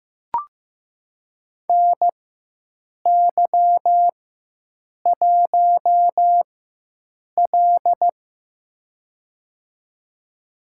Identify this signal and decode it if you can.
Morse code, heard as a keyed tone, about 15 wpm, spaced as for 7 wpm, 700 Hz. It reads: NY1L
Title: USA Call Signs - 4 Characters - 4x Character Spacing - 15wpm